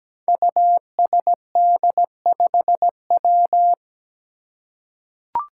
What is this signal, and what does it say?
Morse code, heard as a keyed tone, about 17 wpm, 700 Hz. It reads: USD5W
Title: Five Challenging Letter-Number Combinations 17wpm